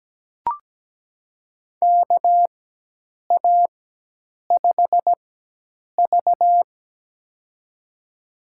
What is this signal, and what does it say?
Morse code, heard as a keyed tone, about 17 wpm, spaced as for 8 wpm, 700 Hz. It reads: KA5V
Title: USA Call Signs - 4 Characters - 4x Character Spacing - 17wpm